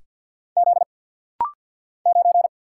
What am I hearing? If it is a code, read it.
Morse code, heard as a keyed tone, about 50 wpm, 700 Hz. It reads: Z 9
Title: Single Letter-Number Alt - Rapid-Fire 50wpm